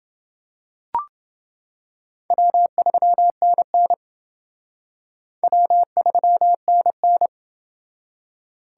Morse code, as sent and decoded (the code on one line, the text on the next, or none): .-- ...-- -. -. / .-- ...-- -. -.
W3NN W3NN